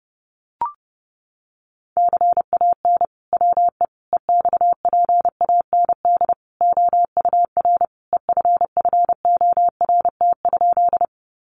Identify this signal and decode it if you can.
Morse code, heard as a keyed tone, about 30 wpm, 700 Hz. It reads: CAN WE EXPAND OUR EFFORT?
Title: Sentences from Top 1400 Words 30wpm